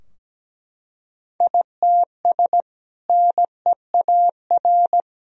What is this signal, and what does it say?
Morse code, heard as a keyed tone, about 17 wpm, 700 Hz. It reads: ITS NEAR
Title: Sentences from Top 1000 Words - Review Part 2 - 17wpm